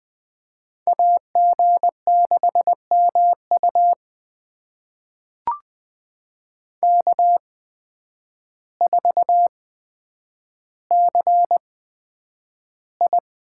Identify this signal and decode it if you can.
Morse code, heard as a keyed tone, about 20 wpm, spaced as for 9 wpm, 700 Hz. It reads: AG6MU K4CI
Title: USA Call Signs - 5 Characters - 8x Character Spacing - 20wpm